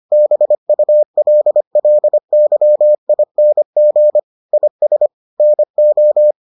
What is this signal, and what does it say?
Morse code, heard as a keyed tone, about 25 wpm, 600 Hz. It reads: BULLYING IS NO